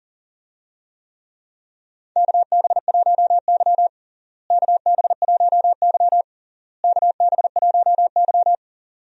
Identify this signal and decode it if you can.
Morse code, heard as a keyed tone, about 40 wpm, 700 Hz. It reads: KB1Y KB1Y KB1Y
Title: USA Call Signs - 4 Characters - Repeated 3x - 40wpm